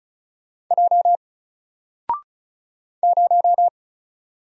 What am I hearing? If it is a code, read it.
Morse code, heard as a keyed tone, about 36 wpm, 700 Hz. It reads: J 0
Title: Single Character (Letter:Number:Punctuation:Prosign) 35wpm